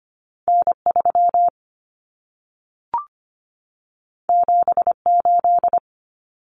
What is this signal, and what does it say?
Morse code, heard as a keyed tone, about 25 wpm, 700 Hz. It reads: N 3 7 8
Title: Common Cut Numbers - 2-Digit 25wpm